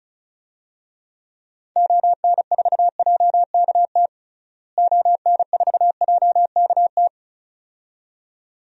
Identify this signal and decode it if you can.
Morse code, heard as a keyed tone, about 36 wpm, 700 Hz. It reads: ON4JKT ON4JKT
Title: DXCC-100 Call Signs - 6 Characters - Repeated 2x - 35wpm